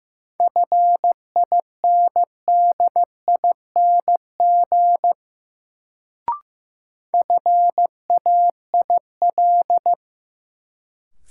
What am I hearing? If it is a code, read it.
Morse code, heard as a keyed tone, about 15 wpm, 700 Hz. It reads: FINDING FAIL
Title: Sentences from Top 1100 Words - New Word Drill Alt - 15wpm